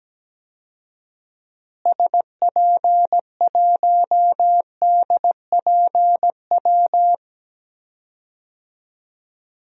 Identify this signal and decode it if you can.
Morse code, heard as a keyed tone, about 17 wpm, 700 Hz. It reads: SP1DPW
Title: DXCC-100 Call Signs - 6 Characters - Repeated 2x - 17wpm